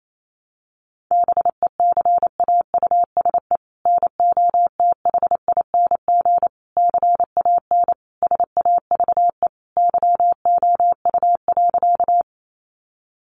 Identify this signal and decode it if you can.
Morse code, heard as a keyed tone, about 28 wpm, 700 Hz. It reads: BECAUSE NOTHING CAN SAVE YOU.
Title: Sentences from Top 1100 Words 28wpm